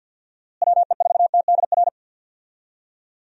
Morse code, heard as a keyed tone, about 50 wpm, 700 Hz. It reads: WE4TDR
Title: USA Call Signs - 6 Characters - Repeated 3x - 50wpm